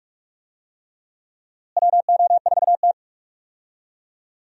Morse code, heard as a keyed tone, about 47 wpm, 700 Hz. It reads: WO4T
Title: USA Call Signs - 4 Characters - Repeated 3x - 45wpm